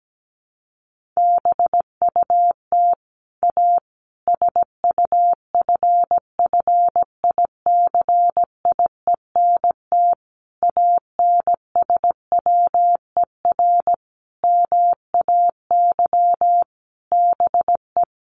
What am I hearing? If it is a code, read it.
Morse code, heard as a keyed tone, about 17 wpm, 700 Hz. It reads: BUT A SUFFICIENT ANSWER MAY BE
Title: Sentences from Top 1400 Words 17wpm